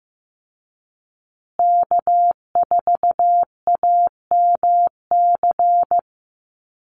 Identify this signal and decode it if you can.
Morse code, heard as a keyed tone, about 15 wpm, 700 Hz. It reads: K4AMC
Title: Call Signs Repeated 2x - 15wpm